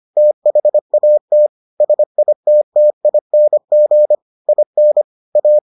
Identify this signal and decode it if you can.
Morse code, heard as a keyed tone, about 25 wpm, 600 Hz. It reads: THAT SITTING IN A